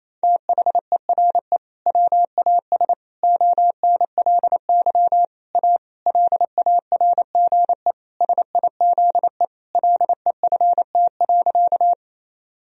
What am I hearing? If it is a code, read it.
Morse code, heard as a keyed tone, about 28 wpm, 690 Hz. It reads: THERE WAS ONLY A LARGE SIZE LEFT.
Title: Sentences from Top 1000 Words - Review Part 2 - 28wpm